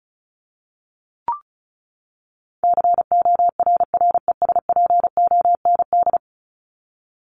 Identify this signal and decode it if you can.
Morse code, heard as a keyed tone, about 34 wpm, 700 Hz. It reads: CORRESPOND
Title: ING Suffix - Top 100 Words with Base Words 35wpm